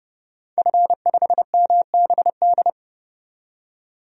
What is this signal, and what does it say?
Morse code, heard as a keyed tone, about 30 wpm, 700 Hz. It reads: F5MBD
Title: DXCC-100 Call Signs - 5 Characters - 30wpm